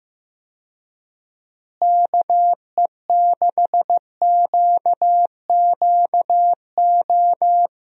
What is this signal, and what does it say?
Morse code, heard as a keyed tone, about 15 wpm, 700 Hz. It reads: KE6QQO
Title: USA Call Signs - 6 Characters - Repeated 3x - 15wpm